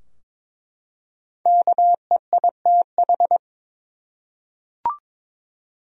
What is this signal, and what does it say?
Morse code, heard as a keyed tone, about 22 wpm, 700 Hz. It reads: KEITH
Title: USA Common Names - Max 5 Letters - 22wpm